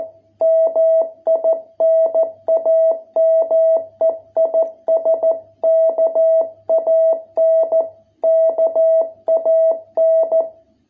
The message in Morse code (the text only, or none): MI NAME IS KAN KAN